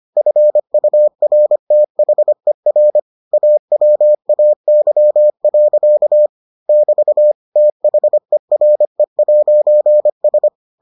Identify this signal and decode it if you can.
Morse code, heard as a keyed tone, about 25 wpm, 600 Hz. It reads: FURTHER AWAY. <BT> THERE'S